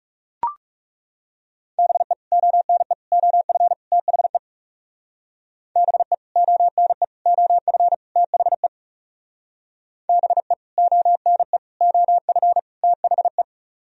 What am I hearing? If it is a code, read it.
Morse code, heard as a keyed tone, about 40 wpm, 700 Hz. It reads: BE ONE OF THE BE ONE OF THE BE ONE OF THE